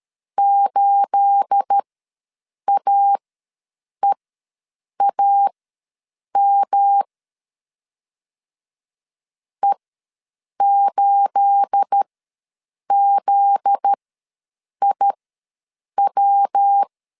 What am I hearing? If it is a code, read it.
Morse code, heard as a keyed tone, about 13 wpm, spaced as for 7 wpm, 800 Hz. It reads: 8AEAM E8ZIW
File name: LEZIONI / 9 28 ws 12wpm 3354